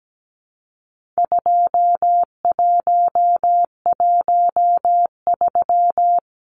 Morse code, heard as a keyed tone, about 17 wpm, 700 Hz. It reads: 2113
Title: Numbers - 4-Digit 17wpm